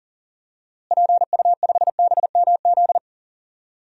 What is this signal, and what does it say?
Morse code, heard as a keyed tone, about 37 wpm, 700 Hz. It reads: PU5BMZ
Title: DXCC-100 Call Signs - 6 Characters - Repeated 3x - 40wpm